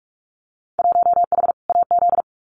Morse code, heard as a keyed tone, about 44 wpm, 700 Hz. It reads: 1H AZ
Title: Winter Field Day Exchanges 45wpm